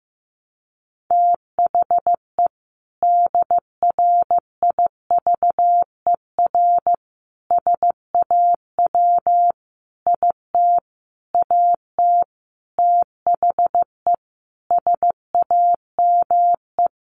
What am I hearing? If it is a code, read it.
Morse code, heard as a keyed tone, about 15 wpm, 700 Hz. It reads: THE DRIVER SAW IT AT THE SAME
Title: Sentences from Top 1000 Words 15wpm